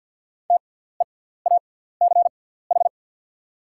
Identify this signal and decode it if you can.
Morse code, heard as a keyed tone, about 48 wpm, spaced as for 17 wpm, 690 Hz. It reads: TEACH
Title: 6x Character Spacing - Top 500 Words - 50wpm